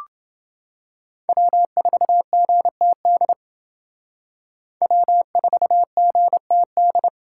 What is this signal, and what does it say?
Morse code, heard as a keyed tone, about 29 wpm, 700 Hz. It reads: W4GTD W4GTD